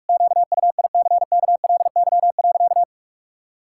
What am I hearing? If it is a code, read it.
Morse code, heard as a keyed tone, about 46 wpm, 700 Hz. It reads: QUICKLY.
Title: Pangram Challenge v2 - 45wpm